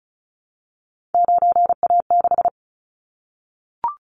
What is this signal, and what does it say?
Morse code, heard as a keyed tone, about 36 wpm, 700 Hz. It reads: 9A6
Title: Callsign Prefixes for Top 50 DXCC - 35wpm